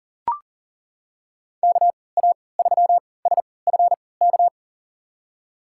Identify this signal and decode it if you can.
Morse code, heard as a keyed tone, about 40 wpm, spaced as for 23 wpm, 700 Hz. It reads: KA3SFK
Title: USA Call Signs - 3x Character Spacing - 40wpm